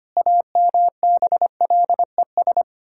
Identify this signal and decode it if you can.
Morse code, heard as a keyed tone, about 25 wpm, 700 Hz. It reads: AMBLES
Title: Pangram Challenge v2 - 25wpm